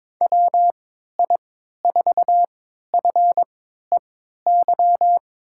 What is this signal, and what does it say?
Morse code, heard as a keyed tone, about 21 wpm, spaced as for 12 wpm, 700 Hz. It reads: WI4FEY
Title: USA Call Signs - 6 Characters - 3x Character Spacing - 22wpm